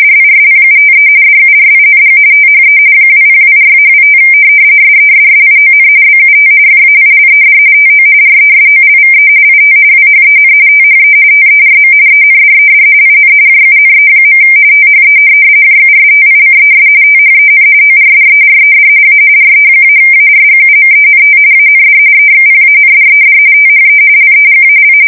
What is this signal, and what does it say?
RTTY, about 45.45 baud. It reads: DISPUTE DURING THE 2024-25 SESSION BUT FAILED TO RESOLVE IT WHEN THE TERM ENDED IN JUNE. THE JUSTICES THEN ORDERED A RE-ARGUMENT AND EXPANDED THE LEGAL
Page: 170 Shift 2125/2295